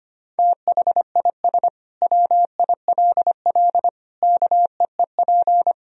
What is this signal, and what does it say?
Morse code, heard as a keyed tone, about 25 wpm, 700 Hz. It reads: THIS WILL KEEP
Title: Sentences from Top 1000 Words - Review Part 2 - 25wpm